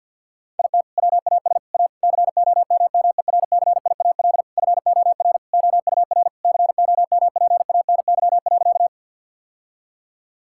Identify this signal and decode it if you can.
Morse code, heard as a keyed tone, about 51 wpm, 700 Hz. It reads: IT WAS A COMMERCIAL FOR OUR COMPANY.
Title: Sentences from Top 1000 Words - Review Part 5 - 50wpm